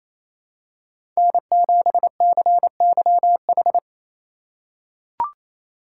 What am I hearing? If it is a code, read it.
Morse code, heard as a keyed tone, about 28 wpm, 700 Hz. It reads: N7CYH
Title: USA Call Signs - 5 Characters - 4x Character Spacing - 28wpm